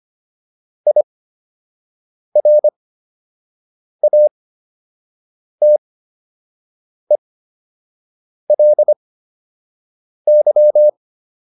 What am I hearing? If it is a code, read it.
Morse code, heard as a keyed tone, about 25 wpm, spaced as for 6 wpm, 600 Hz. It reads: IRATELY